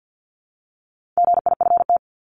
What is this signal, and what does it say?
Morse code, heard as a keyed tone, about 50 wpm, 700 Hz. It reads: GIFT
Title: Top 751-1000 Words 50wpm